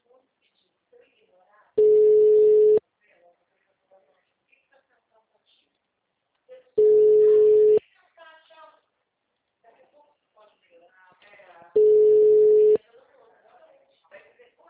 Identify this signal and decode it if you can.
Ringback tone (Europe, 425 Hz), with speech.